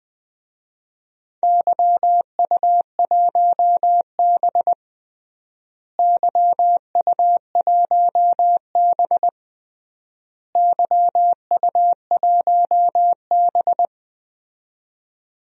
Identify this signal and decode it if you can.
Morse code, heard as a keyed tone, about 20 wpm, 700 Hz. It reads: YU1B YU1B YU1B
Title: DXCC-100 Call Signs - 4 Characters - Repeated 3x - 20wpm